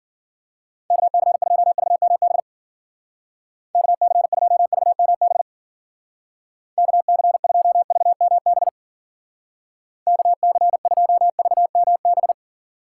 Morse code, heard as a keyed tone, about 50 wpm, 700 Hz. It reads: KC2VMB KC2VMB KC2VMB KC2VMB
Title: Speed Racing - USA 2x3 Call Signs 40wpm